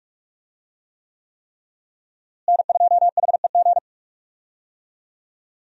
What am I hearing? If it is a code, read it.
Morse code, heard as a keyed tone, about 42 wpm, 700 Hz. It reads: N2HEG
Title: USA Call Signs - 5 Characters - Repeated 3x - 45wpm